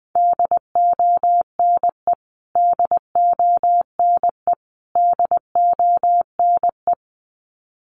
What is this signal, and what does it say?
Morse code, heard as a keyed tone, about 20 wpm, 700 Hz. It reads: DONE DONE DONE
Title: Top 500 Words Repeated 20wpm